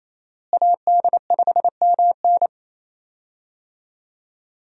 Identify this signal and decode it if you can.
Morse code, heard as a keyed tone, about 28 wpm, 700 Hz. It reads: AD5MN
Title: USA Call Signs - 5 Characters - Repeated 3x - 28wpm